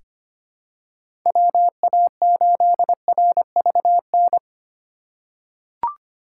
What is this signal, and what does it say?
Morse code, heard as a keyed tone, about 25 wpm, 700 Hz. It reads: WA8RVN
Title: USA Call Signs - 6 Characters - 4x Character Spacing - 25wpm